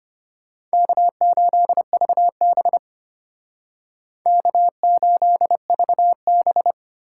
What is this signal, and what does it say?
Morse code, heard as a keyed tone, about 31 wpm, 700 Hz. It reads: K8VB K8VB
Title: Speed Racing - USA 1x2 Call Signs 25wpm